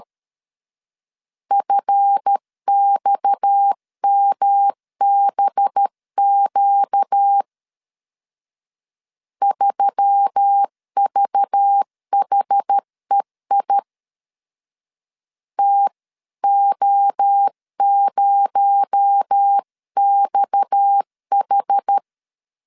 Morse code, heard as a keyed tone, about 13 wpm, 800 Hz. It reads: FXMBQ 3VHEI TO0XH
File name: LEZIONI / 7 GF9Z / 7 GF 12wpm 2122